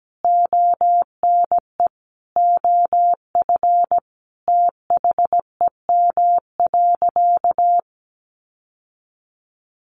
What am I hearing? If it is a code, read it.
Morse code, heard as a keyed tone, about 17 wpm, 700 Hz. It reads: ONE OF THEM.
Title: Sentences from Top 1500 Words 17wpm